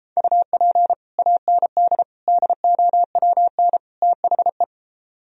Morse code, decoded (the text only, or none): UP AND DOWN THE